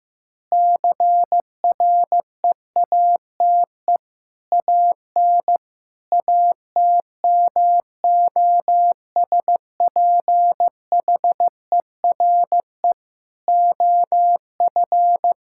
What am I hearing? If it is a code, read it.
Morse code, heard as a keyed tone, about 16 wpm, 700 Hz. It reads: CREATE AN ATMOSPHERE OF
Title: Sentences from Top 1600 Words 15wpm